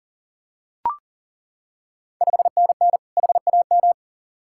Morse code, heard as a keyed tone, about 39 wpm, 690 Hz. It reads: 5NN HAM
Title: Ontario QSO Party Exchange 40wpm